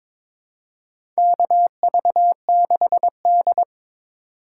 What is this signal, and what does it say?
Morse code, heard as a keyed tone, about 22 wpm, 700 Hz. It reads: KV6D
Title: USA Call Signs - 3x Character Spacing - 22wpm